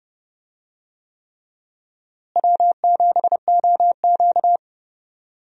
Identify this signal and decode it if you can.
Morse code, heard as a keyed tone, about 32 wpm, 700 Hz. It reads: W7OQ
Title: USA Call Signs - 4 Characters - Repeated 3x - 30wpm